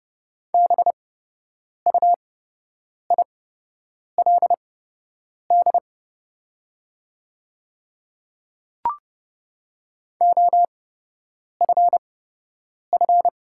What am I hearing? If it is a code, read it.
Morse code, heard as a keyed tone, about 29 wpm, spaced as for 8 wpm, 700 Hz. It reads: BUILD OFF
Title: 8x Character Spacing - Top 200 Words - 30wpm